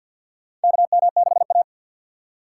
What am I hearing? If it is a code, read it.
Morse code, heard as a keyed tone, about 48 wpm, 700 Hz. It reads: KM6A
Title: USA Call Signs - 6x Character Spacing - 50wpm